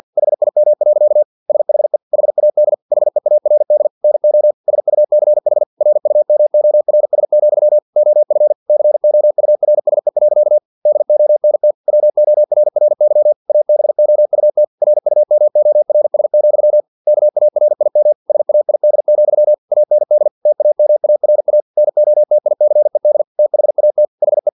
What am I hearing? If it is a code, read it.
Morse code, heard as a keyed tone, about 50 wpm, 600 Hz. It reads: HIM. SHE HAD HEARD NO SUCH RUMORS, OF COURSE. DONT WORRY ABOUT RUMORS, KARIM SAID, AND TAMARA NOTICED THAT HE
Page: Never 49wpm